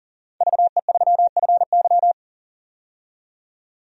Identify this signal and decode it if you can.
Morse code, heard as a keyed tone, about 38 wpm, 700 Hz. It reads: VE3FY
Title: DXCC-100 Call Signs - 5 Characters - 40wpm